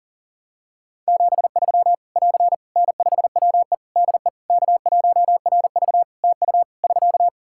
Morse code, heard as a keyed tone, about 40 wpm, 700 Hz. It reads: 73 <AR> N5WE DE K1RV TU <SK>
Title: Practice QSO - Exchange 4 - Ending - 40wpm